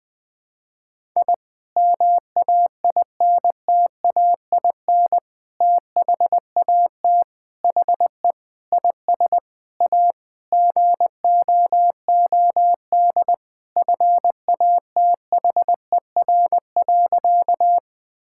Morse code, heard as a keyed tone, about 19 wpm, 700 Hz. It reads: I MAINTAIN THAT HE IS A GOOD FATHER.
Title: Sentences from Top 1000 Words - Review Part 5 - 20wpm